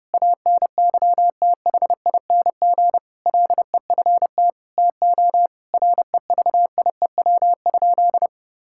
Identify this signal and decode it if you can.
Morse code, heard as a keyed tone, about 32 wpm, 700 Hz. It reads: ANYTHING LEFT TO REVIEW?
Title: Sentences from Top 1000 Words - Review Part 6 - 30wpm